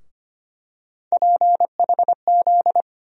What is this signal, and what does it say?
Morse code, heard as a keyed tone, about 25 wpm, 700 Hz. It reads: PHZ
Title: USA Call Sign Postfixes - 3 Letters 25wpm